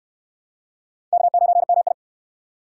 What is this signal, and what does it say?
Morse code, heard as a keyed tone, about 64 wpm, 700 Hz. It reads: K9MI